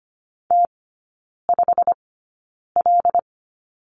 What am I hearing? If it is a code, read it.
Morse code, heard as a keyed tone, about 25 wpm, spaced as for 9 wpm, 700 Hz. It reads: T5L
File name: course / 25:10wpm / 093 Lesson 16e - Letter c - Three Characters 25wpm